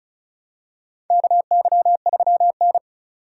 Morse code, heard as a keyed tone, about 34 wpm, 700 Hz. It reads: KY3N
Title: USA Call Signs - 4 Characters - 2x Character Spacing - 35wpm